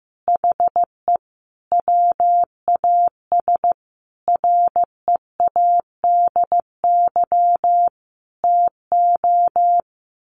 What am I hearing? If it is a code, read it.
Morse code, heard as a keyed tone, about 15 wpm, 700 Hz. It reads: HE WAS READY TO
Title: Sentences from Top 1400 Words 15wpm